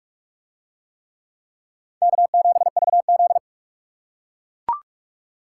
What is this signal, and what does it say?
Morse code, heard as a keyed tone, about 44 wpm, 700 Hz. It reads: K7VZ